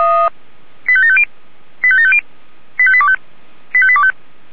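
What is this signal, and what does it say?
Five-tone selective call (ZVEI), heard as quick runs of five tones, tones of about 70 ms. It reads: 85750 85750 85825 85825
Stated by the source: ZVEI five-tone selective call received on SDR.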